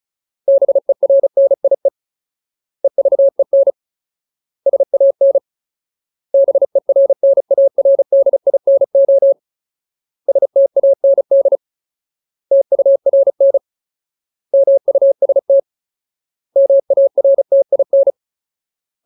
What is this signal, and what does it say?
Morse code, heard as a keyed tone, about 34 wpm, 550 Hz. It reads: BERNIE EVEN SAN BERNARDINO STAND TURN MUST MARTIN
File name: Ultimate Word List / 35WPM Ultimate